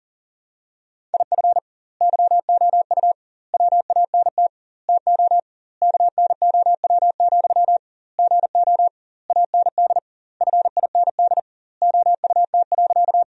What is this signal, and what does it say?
Morse code, heard as a keyed tone, about 40 wpm, 700 Hz. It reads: IF YOU WANT TO KNOW, GO AND FIND OUT.
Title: 2x Word-Spacing - Sentences from Top 200 Words - 40wpm